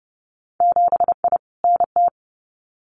Morse code, heard as a keyed tone, about 30 wpm, 700 Hz. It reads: 7I NT
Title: Winter Field Day Exchanges Encore 30wpm